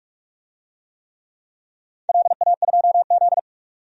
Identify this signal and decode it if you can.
Morse code, heard as a keyed tone, about 44 wpm, 700 Hz. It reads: LA2Z
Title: DXCC-100 Call Signs - 4 Characters - 45wpm